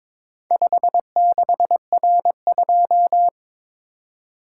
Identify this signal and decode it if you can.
Morse code, heard as a keyed tone, about 22 wpm, 700 Hz. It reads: 56R2
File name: practice / 22wpm / Four Letter-Number Combinations - 22wpm